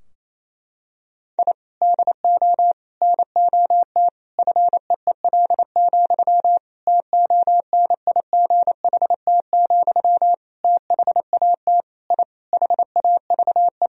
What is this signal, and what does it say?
Morse code, heard as a keyed tone, about 28 wpm, 700 Hz. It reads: I DO NOT FEEL, TONIGHT, THAT I HAVE
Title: Sentences from Top 1000 Words - Review Part 3 - 28wpm